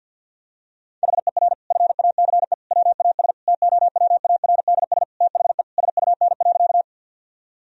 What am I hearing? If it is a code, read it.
Morse code, heard as a keyed tone, about 47 wpm, 700 Hz. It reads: HER FACE WAS TOWARDS THE SUN.
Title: Sentences from Top 1000 Words - Review Part 2 - 50wpm